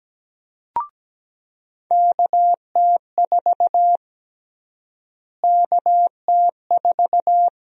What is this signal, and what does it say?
Morse code, heard as a keyed tone, about 17 wpm, 700 Hz. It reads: KT4 KT4
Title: USA Call Sign Prefixes - Repeated 2x - 17wpm